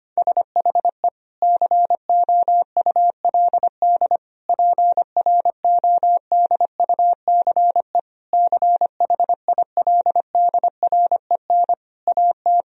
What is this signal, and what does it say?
Morse code, heard as a keyed tone, about 25 wpm, 700 Hz. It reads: SHE COULD PRODUCE CHILDREN AT
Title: Sentences from Top 1000 Words - Review Part 3 - 25wpm